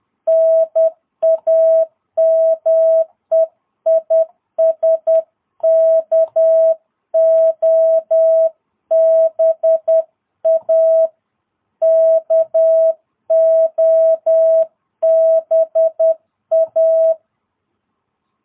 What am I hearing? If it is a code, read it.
Morse code, heard as a keyed tone, about 10 wpm, 660 Hz. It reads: NAME IS KOBA KOBA
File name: jh8cbh / 2022cw